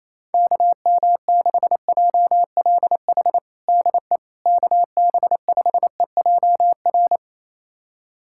Morse code, heard as a keyed tone, about 27 wpm, 700 Hz. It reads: KM6JLH DE KB5EJR
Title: USA Callsign DE Callsign - 6 Characters - 28wpm